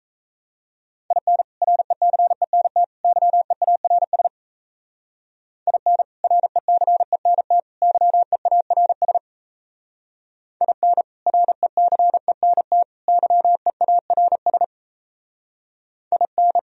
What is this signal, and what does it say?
Morse code, heard as a keyed tone, about 37 wpm, 700 Hz. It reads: IN RECENT YEARS IN RECENT YEARS IN RECENT YEARS IN